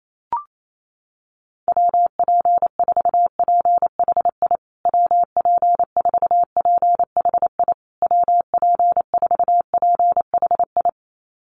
Morse code, heard as a keyed tone, about 28 wpm, 700 Hz. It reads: WP4PHI WP4PHI WP4PHI
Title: USA Call Signs Repeated 3x 28wpm